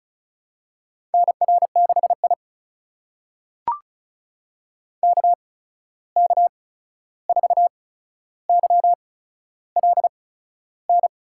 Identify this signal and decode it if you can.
Morse code, heard as a keyed tone, about 36 wpm, spaced as for 13 wpm, 700 Hz. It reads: NR6I KK4YLN
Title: USA Call Signs - 8x Character Spacing - 35wpm